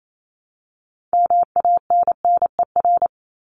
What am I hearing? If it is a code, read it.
Morse code, heard as a keyed tone, about 28 wpm, 700 Hz. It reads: MANNER
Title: Top 1501-1750 Words 28wpm